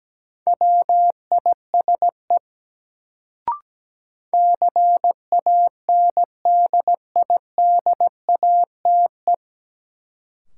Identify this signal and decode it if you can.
Morse code, heard as a keyed tone, about 17 wpm, 700 Hz. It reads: WISE CANDIDATE
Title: Sentences from Top 1600 Words - New Word Drill Alt - 17wpm